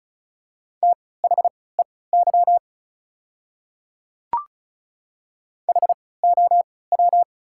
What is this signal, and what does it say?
Morse code, heard as a keyed tone, about 34 wpm, spaced as for 20 wpm, 690 Hz. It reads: THEY HOW
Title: 3x Character Spacing - Top 300 Words - 35wpm